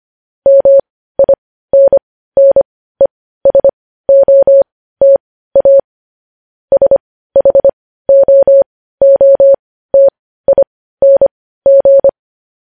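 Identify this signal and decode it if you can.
Morse code, heard as a keyed tone, about 25 wpm, spaced as for 15 wpm, 550 Hz. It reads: MINNESOTA SHOOTING
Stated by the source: Morse news headlines at 15 wpm.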